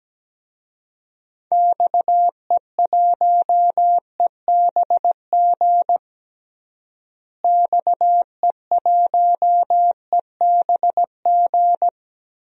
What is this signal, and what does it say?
Morse code, heard as a keyed tone, about 17 wpm, 700 Hz. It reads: XE1EBG XE1EBG
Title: DXCC-100 Call Signs - 6 Characters - Repeated 2x - 17wpm